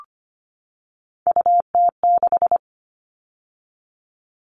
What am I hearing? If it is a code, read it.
Morse code, heard as a keyed tone, about 25 wpm, 700 Hz. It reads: UT6
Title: All Cut Numbers - 3-Digit 25wpm